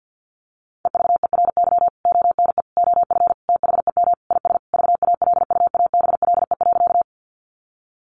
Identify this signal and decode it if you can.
Morse code, heard as a keyed tone, about 50 wpm, 700 Hz. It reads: EVERY ONE OF THEM IS VALUABLE.
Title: Sentences from Top 1100 Words 50wpm